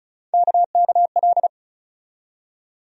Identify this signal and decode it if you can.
Morse code, heard as a keyed tone, about 36 wpm, 700 Hz. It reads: KKL
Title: USA Call Sign Postfixes - 3 Letters 35wpm